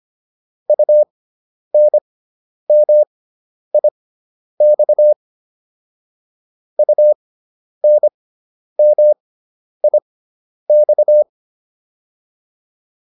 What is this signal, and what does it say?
Morse code, heard as a keyed tone, about 25 wpm, spaced as for 10 wpm, 600 Hz. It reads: UNMIX UNMIX